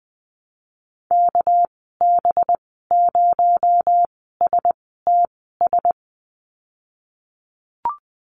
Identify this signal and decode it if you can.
Morse code, heard as a keyed tone, about 20 wpm, spaced as for 14 wpm, 700 Hz. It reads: KB0STS
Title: 245 Lesson 39g - Letter z - Call Signs 20wpm